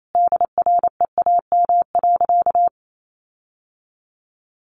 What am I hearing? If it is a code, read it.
Morse code, heard as a keyed tone, about 28 wpm, 700 Hz. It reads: DREAM.
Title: Sentences from Top 1500 Words 28wpm